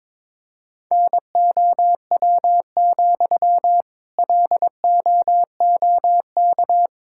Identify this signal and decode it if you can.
Morse code, heard as a keyed tone, about 22 wpm, 700 Hz. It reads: NOW, LOOK
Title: Sentences from Top 1600 Words 22wpm